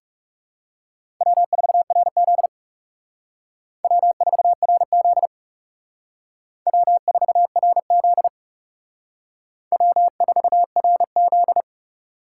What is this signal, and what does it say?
Morse code, heard as a keyed tone, about 37 wpm, 700 Hz. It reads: W4RZ W4RZ W4RZ W4RZ
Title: Speed Racing - USA Call Signs 30wpm